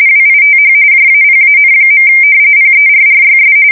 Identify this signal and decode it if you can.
RTTY, about 45.45 baud. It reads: WITH JAPAN. PERRYS
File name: fsk170 / usnavy2